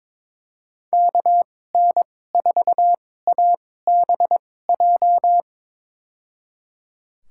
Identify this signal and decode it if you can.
Morse code, heard as a keyed tone, about 22 wpm, spaced as for 16 wpm, 700 Hz. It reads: KN4ABJ
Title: USA Call Signs - 6 Characters - 2x Character Spacing - 22wpm